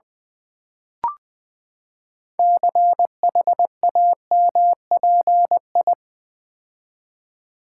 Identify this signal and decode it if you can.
Morse code, heard as a keyed tone, about 20 wpm, 700 Hz. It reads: CHAMPI
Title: Build A Word - 6 Letters At A Time - 12-Letter Words - 20wpm